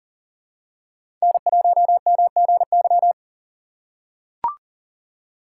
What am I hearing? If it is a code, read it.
Morse code, heard as a keyed tone, about 40 wpm, 700 Hz. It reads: N1MGY
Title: USA Call Signs - 5 Characters - 3x Character Spacing - 40wpm